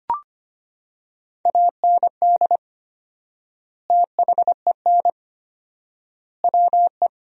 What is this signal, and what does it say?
Morse code, heard as a keyed tone, about 25 wpm, 700 Hz. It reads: AND THEN WE
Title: 4x Word-Spacing - Sentences from Top 300 Words - 25wpm